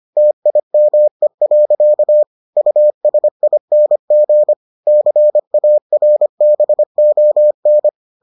Morse code, heard as a keyed tone, about 25 wpm, 600 Hz. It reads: TIME. USING CARBON